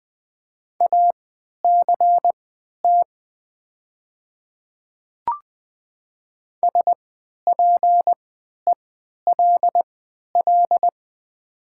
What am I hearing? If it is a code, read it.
Morse code, heard as a keyed tone, about 20 wpm, spaced as for 11 wpm, 700 Hz. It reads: ACT SPELL
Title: 3x Character Spacing - Top 500 Words - 20wpm